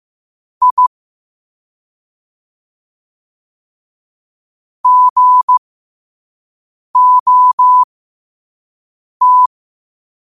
Morse code, heard as a keyed tone, about 15 wpm, spaced as for 5 wpm, 1000 Hz. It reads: I GOT
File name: qotdfiles / 5wpm